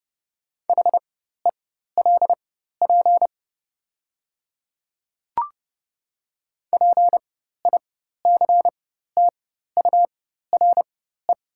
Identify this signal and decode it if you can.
Morse code, heard as a keyed tone, about 30 wpm, spaced as for 14 wpm, 700 Hz. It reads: HELP PICTURE
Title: 4x Character Spacing - Top 200 Words - 30wpm